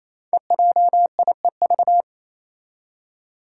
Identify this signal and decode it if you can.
Morse code, heard as a keyed tone, about 27 wpm, 700 Hz. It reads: EJIEV
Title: Five Challenging Letter-Number Combinations 28wpm